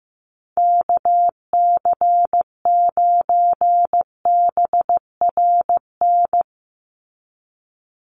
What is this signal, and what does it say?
Morse code, heard as a keyed tone, about 15 wpm, 700 Hz. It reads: KC9BRN
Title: USA Call Signs 15wpm